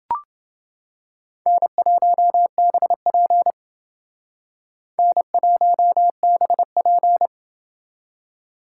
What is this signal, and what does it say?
Morse code, heard as a keyed tone, about 28 wpm, 700 Hz. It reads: N1BP N1BP